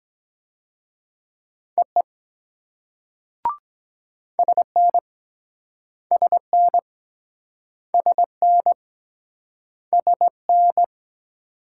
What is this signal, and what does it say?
Morse code, heard as a keyed tone, about 23 wpm, 690 Hz. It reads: EE SN SN SN SN